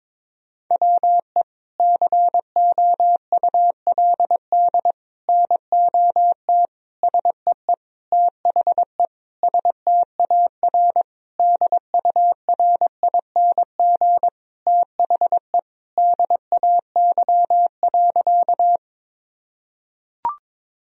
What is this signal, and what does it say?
Morse code, heard as a keyed tone, about 21 wpm, 700 Hz. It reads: WE COULD NOT SEE THE STAR DURING THE DAY.
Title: Sentences from Top 1000 Words - Review Part 2 - 22wpm